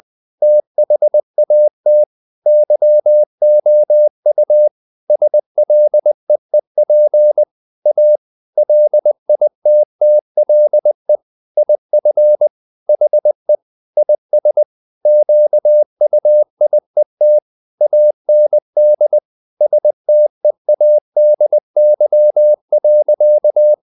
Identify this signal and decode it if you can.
Morse code, heard as a keyed tone, about 20 wpm, 600 Hz. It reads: THAT YOU SLEEP A LITTLE IF HE IS QUIET AND STEADY.
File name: the old man and the sea part 2 20wpm